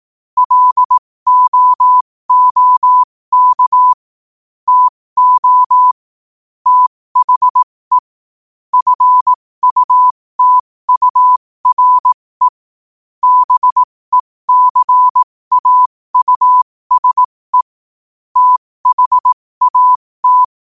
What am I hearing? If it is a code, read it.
Morse code, heard as a keyed tone, about 18 wpm, spaced as for 15 wpm, 1000 Hz. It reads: LOOK TO THE FUTURE BECAUSE THAT
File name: qotdfiles / 15wpm